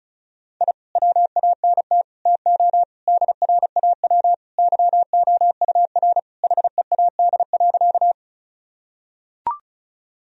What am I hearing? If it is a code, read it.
Morse code, heard as a keyed tone, about 34 wpm, 700 Hz. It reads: I WANT TO DRAW YOUR HEAD.
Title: Sentences from Top 1000 Words - Review Part 1 - 35wpm